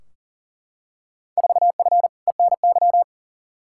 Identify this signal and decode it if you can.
Morse code, heard as a keyed tone, about 40 wpm, 700 Hz. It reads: 4F ENY
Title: Field Day Exchanges - 40wpm